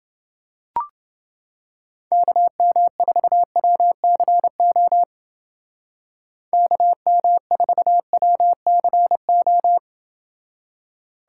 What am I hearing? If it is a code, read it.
Morse code, heard as a keyed tone, about 29 wpm, 700 Hz. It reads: KM4WCO KM4WCO